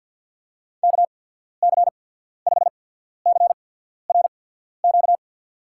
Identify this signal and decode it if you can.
Morse code, heard as a keyed tone, about 50 wpm, spaced as for 14 wpm, 700 Hz. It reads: KC5CRQ
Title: USA Call Signs - 6 Characters - 8x Character Spacing - 50wpm